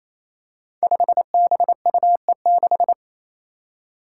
Morse code, heard as a keyed tone, about 28 wpm, 700 Hz. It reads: 5BUE6
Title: Five Challenging Letter-Number Combinations 28wpm